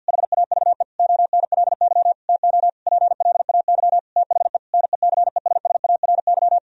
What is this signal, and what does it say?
Morse code, heard as a keyed tone, about 49 wpm, 700 Hz. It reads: HAVE ONLY TO PLAY THE NECESSARY
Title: Sentences from Top 1000 Words - Review Part 1 - 50wpm